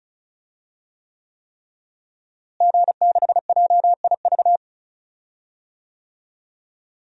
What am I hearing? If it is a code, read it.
Morse code, heard as a keyed tone, about 36 wpm, 700 Hz. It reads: G6JIV
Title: DXCC-100 Call Signs - 5 Characters - Repeated 3x - 35wpm